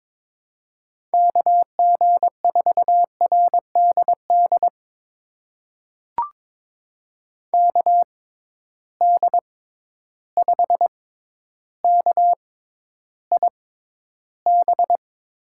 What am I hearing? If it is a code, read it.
Morse code, heard as a keyed tone, about 21 wpm, spaced as for 11 wpm, 700 Hz. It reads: KG4RDD KD5KIB
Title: USA Call Signs - 6 Characters - 6x Character Spacing - 22wpm